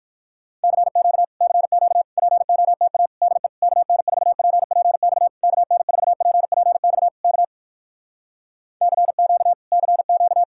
Morse code, heard as a keyed tone, about 64 wpm, 700 Hz. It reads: CQ CQ POTA DE KN4PPX KN4PPX K CQ CQ